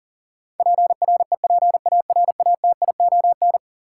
Morse code, heard as a keyed tone, about 40 wpm, 700 Hz. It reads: PREPARATION
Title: Sentences from Top 1400 Words - New Word Drill Alt - 40wpm